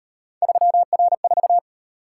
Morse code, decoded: 3R4

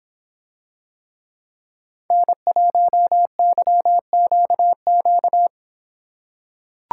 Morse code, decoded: N1YQQ

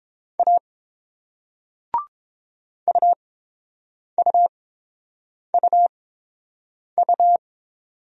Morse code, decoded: A UUUU

